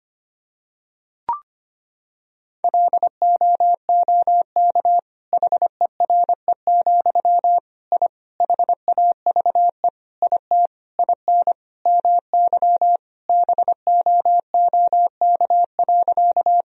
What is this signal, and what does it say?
Morse code, heard as a keyed tone, about 25 wpm, 700 Hz. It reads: LOOK HERE, I HAVE IT IN MY BOOK.
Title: Sentences from Top 1000 Words - Review Part 1 - 25wpm